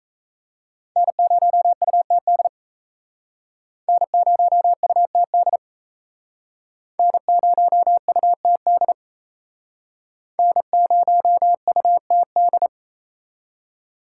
Morse code, heard as a keyed tone, about 34 wpm, 700 Hz. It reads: N0UTD N0UTD N0UTD N0UTD
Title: Speed Racing - USA 1x3 Call Signs 28wpm